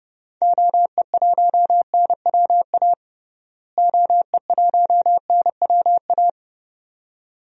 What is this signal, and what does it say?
Morse code, heard as a keyed tone, about 32 wpm, 700 Hz. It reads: OE1NWA OE1NWA
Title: DXCC-100 Call Signs - 6 Characters - Repeated 2x - 30wpm